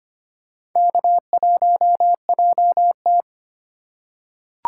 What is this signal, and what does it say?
Morse code, heard as a keyed tone, about 25 wpm, 700 Hz. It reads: K1JT
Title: USA Call Signs - 4x Character Spacing - 25wpm